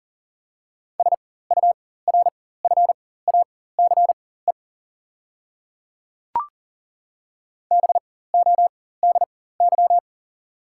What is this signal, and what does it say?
Morse code, heard as a keyed tone, about 40 wpm, spaced as for 19 wpm, 700 Hz. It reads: SURFACE BODY
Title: 4x Character Spacing - Top 500 Words - 40wpm